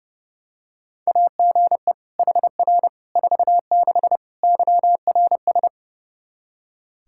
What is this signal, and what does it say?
Morse code, heard as a keyed tone, about 32 wpm, 700 Hz. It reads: AGE HR 46 YRS
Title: Practice QSO - Exchange 3 - More Info - 30wpm